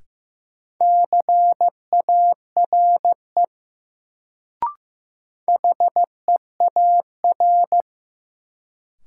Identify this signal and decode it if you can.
Morse code, heard as a keyed tone, about 15 wpm, 700 Hz. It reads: CARE HEAR
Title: Sentences from Top 300 Words - New Word Drill - 15wpm